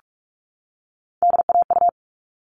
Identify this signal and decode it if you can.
Morse code, heard as a keyed tone, about 44 wpm, 700 Hz. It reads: DAU
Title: All Cut Numbers - 3-Digit 45wpm